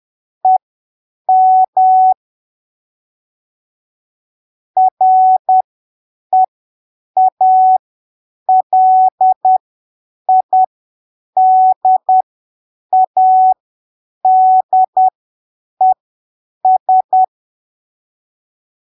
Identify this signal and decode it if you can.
Morse code, heard as a keyed tone, about 10 wpm, spaced as for 7 wpm, 750 Hz. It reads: EM REALIDADES
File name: arquivos / texto / texto 15 10ppm